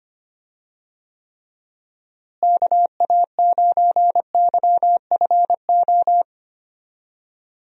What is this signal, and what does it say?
Morse code, heard as a keyed tone, about 25 wpm, 700 Hz. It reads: KA9YFO
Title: USA Call Signs - 6 Characters - Repeated 2x - 25wpm